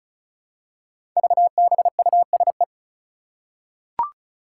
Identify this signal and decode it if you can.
Morse code, heard as a keyed tone, about 36 wpm, 700 Hz. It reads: VBUSE
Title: Five Challenging Letter-Number Combinations 35wpm